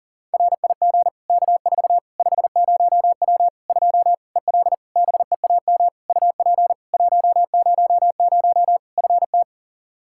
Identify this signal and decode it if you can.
Morse code, heard as a keyed tone, about 40 wpm, 700 Hz. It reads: RIG K4 50W 2 EL BEAM UP 100 FT
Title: Practice QSO - Exchange 2 - Info - 40wpm